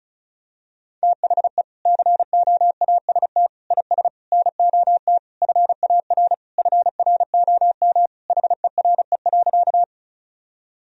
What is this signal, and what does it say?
Morse code, heard as a keyed tone, about 34 wpm, 700 Hz. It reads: THE COAST IS NOT FAR FROM HERE.
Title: Sentences from Top 1000 Words - Review Part 4 - 35wpm